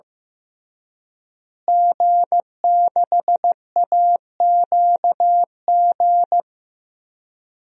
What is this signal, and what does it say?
Morse code, heard as a keyed tone, about 16 wpm, 700 Hz. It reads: G6AQG
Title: DXCC-100 Call Signs - 5 Characters - Repeated 3x - 15wpm